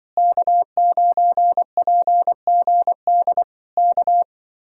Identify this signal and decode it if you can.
Morse code, heard as a keyed tone, about 25 wpm, 700 Hz. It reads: K9PGD K
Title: Speed-Racing Common QSO Phrases - 20wpm